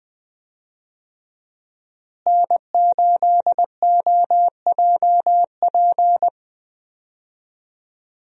Morse code, heard as a keyed tone, about 21 wpm, 700 Hz. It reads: N8OJP
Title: USA Call Signs - 5 Characters - Repeated 3x - 20wpm